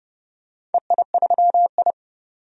Morse code, heard as a keyed tone, about 30 wpm, 700 Hz. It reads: EI3I
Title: DXCC-100 Call Signs - 4 Characters - 30wpm